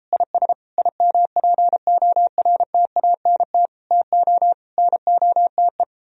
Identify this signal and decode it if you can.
Morse code, heard as a keyed tone, about 33 wpm, 700 Hz. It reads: IS IMPORTANT TO NOTE